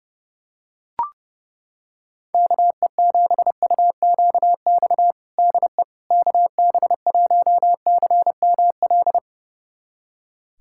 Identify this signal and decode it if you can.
Morse code, heard as a keyed tone, about 30 wpm, 700 Hz. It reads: KE7UQX DE KB1CML
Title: USA Callsign DE Callsign 30wpm